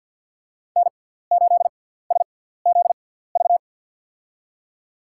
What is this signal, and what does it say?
Morse code, heard as a keyed tone, about 53 wpm, spaced as for 17 wpm, 700 Hz. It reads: N8SZV
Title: USA Call Signs - 6x Character Spacing - 50wpm